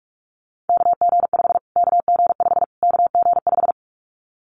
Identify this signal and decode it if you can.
Morse code, heard as a keyed tone, about 44 wpm, 700 Hz. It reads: KG5 KG5 KG5
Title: Callsign Prefixes for Top 25 DXCC - Repeated 3x - 45wpm